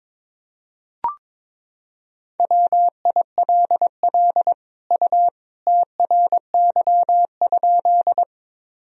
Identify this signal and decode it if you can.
Morse code, heard as a keyed tone, about 22 wpm, 700 Hz. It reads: WILL U TRY?
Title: Sentences from Top 200 Words and Abbreviations 22wpm